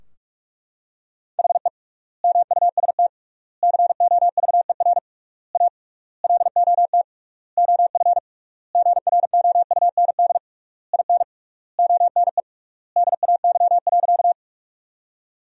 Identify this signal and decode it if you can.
Morse code, heard as a keyed tone, about 41 wpm, 700 Hz. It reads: HE MUST COVER A LOT OF GROUND IN ONE DAY.
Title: 3x Word-Spacing - Sentences from Top 500 Words - 45wpm